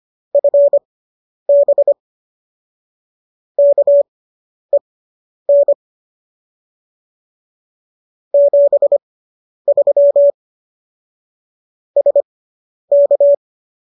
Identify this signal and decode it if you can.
Morse code, heard as a keyed tone, about 25 wpm, spaced as for 10 wpm, 570 Hz. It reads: FB KEN 73 SK